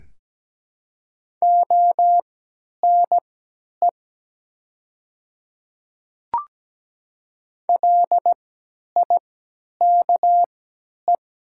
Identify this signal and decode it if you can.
Morse code, heard as a keyed tone, about 17 wpm, spaced as for 10 wpm, 700 Hz. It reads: ONE LIKE
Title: 3x Character Spacing - Top 200 Words - 17wpm